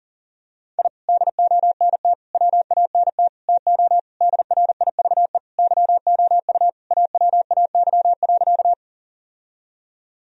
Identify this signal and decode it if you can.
Morse code, heard as a keyed tone, about 40 wpm, 700 Hz. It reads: I DONT WANT TO DRIVE YOU AWAY.
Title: Sentences from Top 1000 Words - Review Part 2 - 40wpm